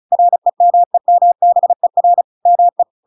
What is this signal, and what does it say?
Morse code, heard as a keyed tone, about 34 wpm, 700 Hz. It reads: REMEMBER ME
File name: Pinocchio / Pinocchio ch7 35 35 ik1hge